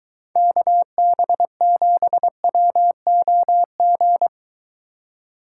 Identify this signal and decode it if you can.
Morse code, heard as a keyed tone, about 23 wpm, 700 Hz. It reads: KB7WOG